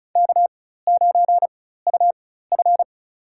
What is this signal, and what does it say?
Morse code, heard as a keyed tone, about 36 wpm, spaced as for 16 wpm, 700 Hz. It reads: K9UF
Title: USA Call Signs - 4x Character Spacing - 35wpm